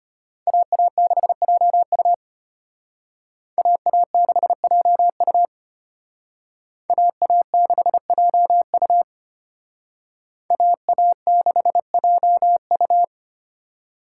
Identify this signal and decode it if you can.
Morse code, heard as a keyed tone, about 32 wpm, 700 Hz. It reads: AA6JU AA6JU AA6JU AA6JU